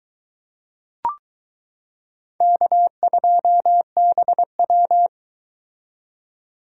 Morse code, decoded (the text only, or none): K2BW